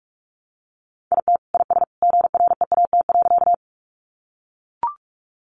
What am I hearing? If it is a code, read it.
Morse code, heard as a keyed tone, about 44 wpm, 700 Hz. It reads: IT IS GREAT.
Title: Sentences from Top 200 Words 45wpm